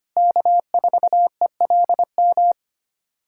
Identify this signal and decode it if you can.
Morse code, heard as a keyed tone, about 25 wpm, 700 Hz. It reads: K4ELM
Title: USA Callsign DE Callsign - 5 Characters - 25wpm